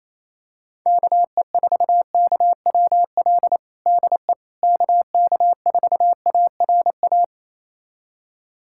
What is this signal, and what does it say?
Morse code, heard as a keyed tone, about 27 wpm, 700 Hz. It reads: KE4KWL DE KK4ARA
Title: USA Callsign DE Callsign - 6 Characters - 28wpm